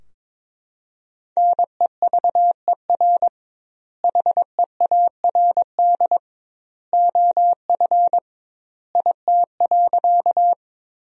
Morse code, heard as a keyed tone, about 21 wpm, 700 Hz. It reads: NEVER HEARD OF IT.
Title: 2x Word-Spacing - Sentences from Top 400 Words - 22wpm